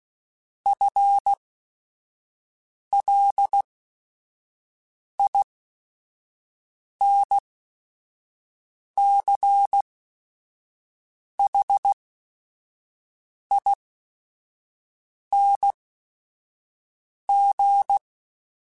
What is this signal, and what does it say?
Morse code, heard as a keyed tone, about 16 wpm, spaced as for 5 wpm, 800 Hz. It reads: FLINCHING
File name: Special INGs / Special INGs-mp3s / Sngl Ing @5